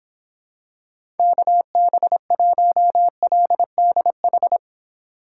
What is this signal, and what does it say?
Morse code, heard as a keyed tone, about 26 wpm, 700 Hz. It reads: KB1LDH